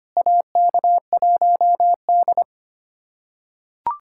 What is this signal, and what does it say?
Morse code, heard as a keyed tone, about 25 wpm, 700 Hz. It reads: AK1D
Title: USA Callsign DE Callsign - 4 Characters - 25wpm